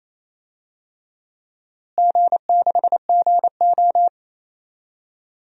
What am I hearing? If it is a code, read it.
Morse code, heard as a keyed tone, about 28 wpm, 700 Hz. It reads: G6GO
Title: DXCC-100 Call Signs - 4 Characters - Repeated 3x - 28wpm